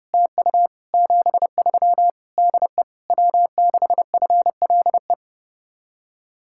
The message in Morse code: TU 73 DE W6FLE

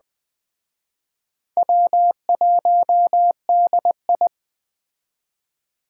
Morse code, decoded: W1DI

